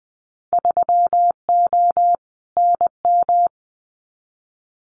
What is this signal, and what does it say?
Morse code, heard as a keyed tone, about 19 wpm, 700 Hz. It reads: 3O NM
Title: Winter Field Day Exchanges Encore 20wpm